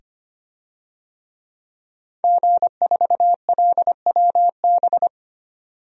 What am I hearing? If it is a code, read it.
Morse code, heard as a keyed tone, about 25 wpm, 700 Hz. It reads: G4LWB
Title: DXCC-100 Call Signs - 5 Characters - 25wpm